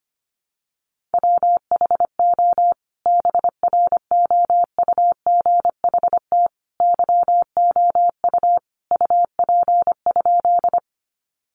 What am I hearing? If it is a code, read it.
Morse code, heard as a keyed tone, about 25 wpm, 700 Hz. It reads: WHO BROUGHT YOU UP?
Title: Sentences from Top 500 Words 25wpm